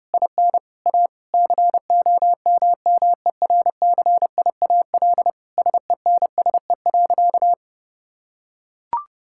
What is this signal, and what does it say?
Morse code, heard as a keyed tone, about 30 wpm, 700 Hz. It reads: IN A COMMERCIAL SENSE.
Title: Sentences from Top 1000 Words - Review Part 5 - 30wpm